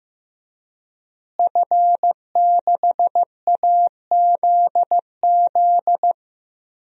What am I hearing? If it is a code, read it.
Morse code, heard as a keyed tone, about 15 wpm, 700 Hz. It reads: F6AZZ
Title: DXCC-100 Call Signs - 5 Characters - Repeated 3x - 15wpm